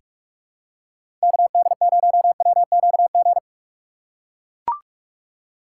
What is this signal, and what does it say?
Morse code, heard as a keyed tone, about 43 wpm, 700 Hz. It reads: KD0WQG
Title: USA Call Signs - 3x Character Spacing - 45wpm